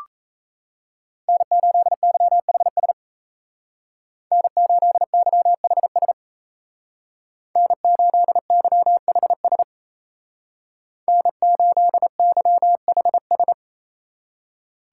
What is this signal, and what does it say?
Morse code, heard as a keyed tone, about 34 wpm, 700 Hz. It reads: N8YHS N8YHS N8YHS N8YHS
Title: Speed Racing - USA 1x3 Call Signs 28wpm